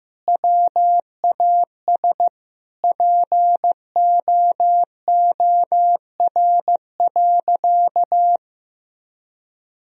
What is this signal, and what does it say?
Morse code, heard as a keyed tone, about 15 wpm, 700 Hz. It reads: WAS POOR.
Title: Sentences from Top 1000 Words - Review Part 3 - 15wpm